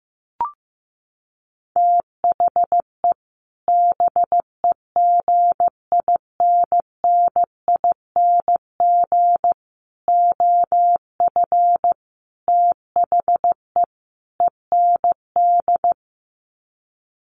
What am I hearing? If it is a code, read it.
Morse code, heard as a keyed tone, about 15 wpm, 700 Hz. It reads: THE BEGINNING OF THE END
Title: Sets of 5 Words Encore2 15wpm